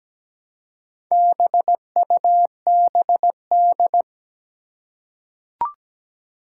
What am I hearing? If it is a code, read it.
Morse code, heard as a keyed tone, about 17 wpm, 700 Hz. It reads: BUBD
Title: Four Challenging Letter-Number Combinations 17wpm